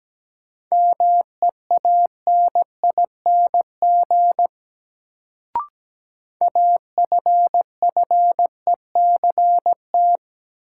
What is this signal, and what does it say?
Morse code, heard as a keyed tone, about 17 wpm, 700 Hz. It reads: MEANING AFFECT
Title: Sentences from Top 900 Words - New Word Drill - 17wpm